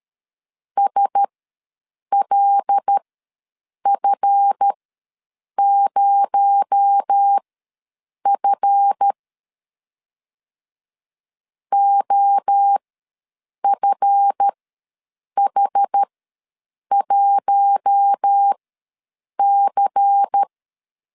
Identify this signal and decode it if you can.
Morse code, heard as a keyed tone, about 13 wpm, spaced as for 7 wpm, 800 Hz. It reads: SLF0F OFH1C
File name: LEZIONI / 7 GF9Z / 7 GF ws 12wpm 3140